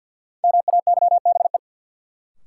Morse code, heard as a keyed tone, about 52 wpm, 700 Hz. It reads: MAYBE
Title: Sentences from Top 1000 Words - New Word Drill - 50wpm